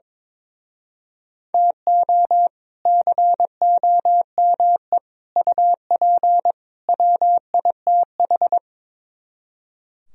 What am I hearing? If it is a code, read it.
Morse code, heard as a keyed tone, about 23 wpm, 700 Hz. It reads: TO COME UP WITH
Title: Speed-Racing - Sets of 4 Words 22wpm